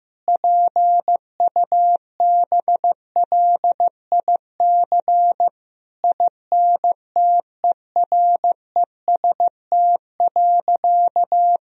Morse code, heard as a keyed tone, about 15 wpm, 700 Hz. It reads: PUBLIC INTEREST.
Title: Sentences from Top 1000 Words - Review Part 4 - 15wpm